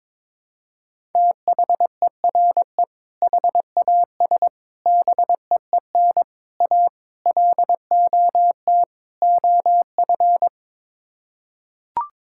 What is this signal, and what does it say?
Morse code, heard as a keyed tone, about 22 wpm, 700 Hz. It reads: THERE HAS BEEN A LOT OF
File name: practice / 22wpm / Sets of 6 Words - 22wpm